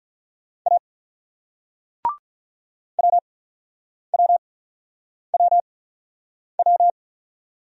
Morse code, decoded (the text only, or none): A WWWW